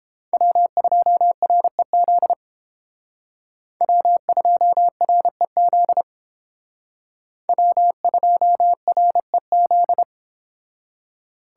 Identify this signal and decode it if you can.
Morse code, heard as a keyed tone, about 31 wpm, 700 Hz. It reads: W2REZ W2REZ W2REZ